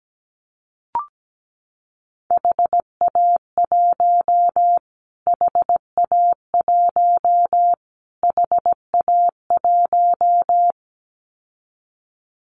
Morse code, heard as a keyed tone, about 17 wpm, 700 Hz. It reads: HA1 HA1 HA1
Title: Callsign Prefixes for Top 25 DXCC - Repeated 3x - 17wpm